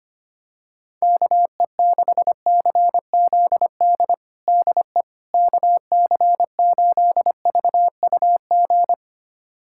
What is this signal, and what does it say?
Morse code, heard as a keyed tone, about 25 wpm, 700 Hz. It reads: KE6CZD DE KC8VUG
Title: USA Callsign DE Callsign - 6 Characters - 25wpm